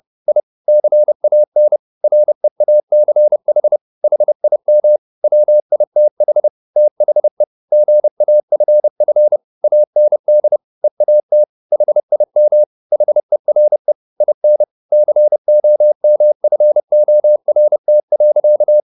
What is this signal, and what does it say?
Morse code, heard as a keyed tone, about 30 wpm, 600 Hz. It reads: I CAN REACH HIM WITH THE GAFF AND EAT HIM HERE IN COMFORT.
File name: the old man and the sea / the old man and the sea 1 30wpm